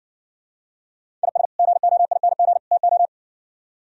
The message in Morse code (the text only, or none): IS GOING TO